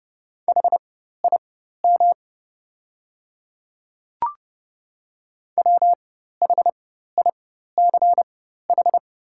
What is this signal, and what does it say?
Morse code, heard as a keyed tone, about 30 wpm, spaced as for 14 wpm, 690 Hz. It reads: HIM WHICH
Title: 4x Character Spacing - Top 100 Words - 30wpm